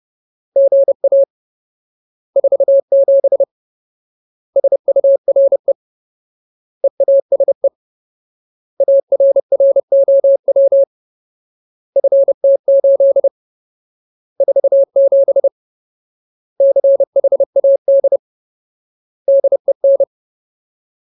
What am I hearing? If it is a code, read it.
Morse code, heard as a keyed tone, about 30 wpm, 550 Hz. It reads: GA 47 SURE EASE ARROW FT8 47 CHAD DEN
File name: Ultimate Word List / 30WPM Ultimate